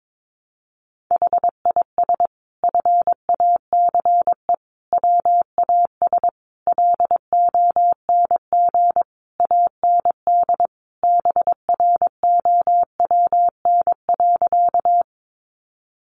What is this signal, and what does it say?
Morse code, heard as a keyed tone, about 22 wpm, 700 Hz. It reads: HIS FACE WAS LONG AND BROWN.
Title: Sentences from Top 1200 Words 22wpm